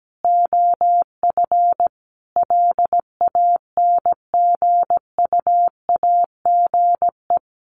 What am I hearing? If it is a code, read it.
Morse code, heard as a keyed tone, about 17 wpm, 700 Hz. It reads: OF LANGUAGE
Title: Sentences from Top 500 Words 17wpm